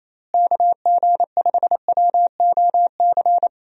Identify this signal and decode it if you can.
Morse code, heard as a keyed tone, about 27 wpm, 700 Hz. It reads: KG5WOC
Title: USA Call Signs - 6 Characters - Repeated 3x - 28wpm